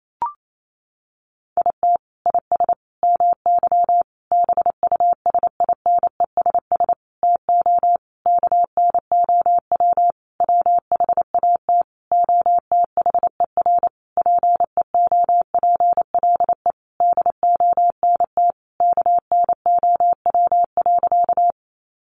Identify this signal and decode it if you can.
Morse code, heard as a keyed tone, about 28 wpm, 700 Hz. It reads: IT IS MY BUSINESS TO KNOW WHAT OTHER PEOPLE DONT KNOW.
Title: Sentences from Top 600 Words 28wpm